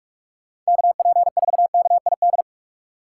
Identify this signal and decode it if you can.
Morse code, heard as a keyed tone, about 60 wpm, spaced as for 52 wpm, 700 Hz. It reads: KP4KID
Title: Speed Racing - USA 2x3 Call Signs 45wpm